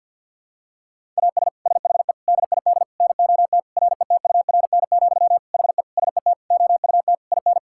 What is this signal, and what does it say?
Morse code, heard as a keyed tone, about 51 wpm, 700 Hz. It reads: AS SHE DID NOT RETURN, HE SET OUT IN
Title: Sentences from Top 1000 Words - Review Part 5 - 50wpm